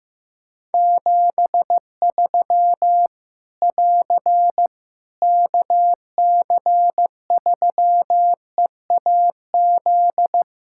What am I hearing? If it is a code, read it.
Morse code, heard as a keyed tone, about 14 wpm, 700 Hz. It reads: 73 <AR> KC3EAZ
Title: Practice QSO - Exchange 4 - Ending - 15wpm